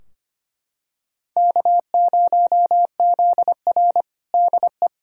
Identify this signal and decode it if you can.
Morse code, heard as a keyed tone, about 25 wpm, 700 Hz. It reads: K0ZR DE
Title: Practice QSO - Exchange 4 - Ending - 25wpm